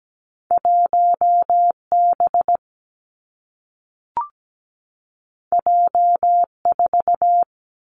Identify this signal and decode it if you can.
Morse code, heard as a keyed tone, about 17 wpm, 700 Hz. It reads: 1B J4
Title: Two Challenging Letter-Number Combinations 17wpm